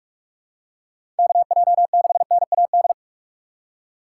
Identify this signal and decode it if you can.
Morse code, heard as a keyed tone, about 46 wpm, 700 Hz. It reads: KJ6NAD
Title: USA Call Signs - 6 Characters - 2x Character Spacing - 45wpm